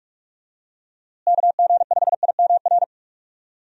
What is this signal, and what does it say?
Morse code, heard as a keyed tone, about 46 wpm, 700 Hz. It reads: KG5IMR
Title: USA Call Signs - 6 Characters - Repeated 2x - 45wpm